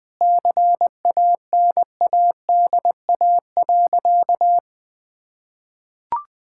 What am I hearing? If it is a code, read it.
Morse code, heard as a keyed tone, about 21 wpm, 700 Hz. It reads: CANADA.
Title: Sentences with 4-Word Phrases - Spoken Beginning - 20wpm